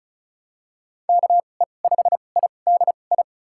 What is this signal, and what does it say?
Morse code, heard as a keyed tone, about 34 wpm, spaced as for 25 wpm, 690 Hz. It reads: KE5IDI
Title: USA Call Signs - 6 Characters - 2x Character Spacing - 35wpm